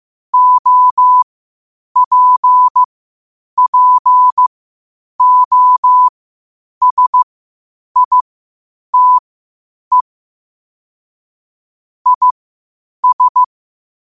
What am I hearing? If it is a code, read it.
Morse code, heard as a keyed tone, about 15 wpm, spaced as for 8 wpm, 1000 Hz. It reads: OPPOSITE IS
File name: qotdfiles / 8wpm